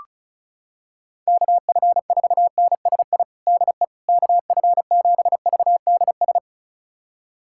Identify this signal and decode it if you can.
Morse code, heard as a keyed tone, about 36 wpm, 700 Hz. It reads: KF4NSI DE KF7VDS
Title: USA Callsign DE Callsign - 6 Characters - 35wpm